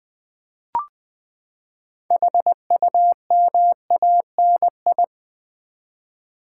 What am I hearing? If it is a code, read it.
Morse code, heard as a keyed tone, about 20 wpm, 700 Hz. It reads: HUMANI
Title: Build A Word - 6 Letters At A Time - 12-Letter Words - 20wpm